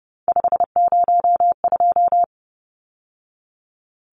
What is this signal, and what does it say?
Morse code, heard as a keyed tone, about 30 wpm, 700 Hz. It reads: 502
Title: Numbers - 3-Digit 30wpm